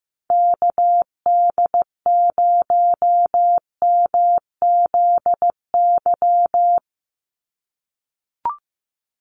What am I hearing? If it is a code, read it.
Morse code, heard as a keyed tone, about 15 wpm, 700 Hz. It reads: KD0MZY
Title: USA Call Signs 15wpm